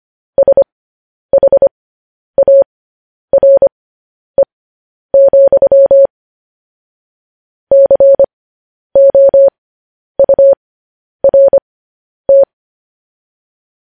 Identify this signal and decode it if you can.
Morse code, heard as a keyed tone, about 25 wpm, spaced as for 10 wpm, 550 Hz. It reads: SHARE, COURT
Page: news headlines in morse code at 10 wpm